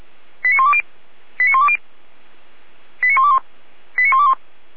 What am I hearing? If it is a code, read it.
Five-tone selective call (ZVEI), heard as quick runs of five tones, tones of about 70 ms. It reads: 89120 89120 89121 89121